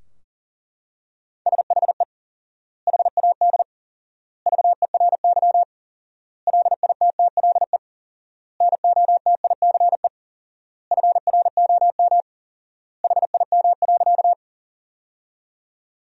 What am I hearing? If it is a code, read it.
Morse code, heard as a keyed tone, about 40 wpm, 700 Hz. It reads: SHE HAD VERY LITTLE NOTICE FROM HIM.
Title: 4x Word-Spacing - Sentences from Top 400 Words - 40wpm